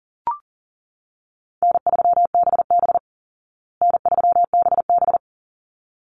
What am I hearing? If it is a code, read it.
Morse code, heard as a keyed tone, about 40 wpm, 700 Hz. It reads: N3BB N3BB
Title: Easy Call Signs Repeated 2x - 40wpm